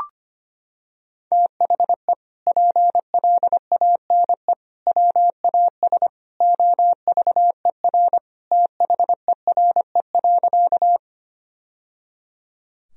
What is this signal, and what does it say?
Morse code, heard as a keyed tone, about 25 wpm, 700 Hz. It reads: THE PLANE WAS OVER THERE.
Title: Sentences from Top 1000 Words - Review Part 3 - 25wpm